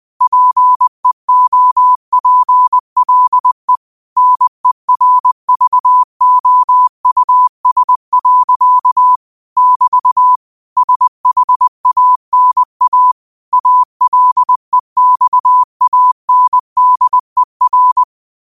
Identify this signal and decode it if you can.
Morse code, heard as a keyed tone, about 20 wpm, 1000 Hz. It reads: PEOPLE NERVOUS. <BT> SHANA ALEXANDER
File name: qotdfiles / 20wpm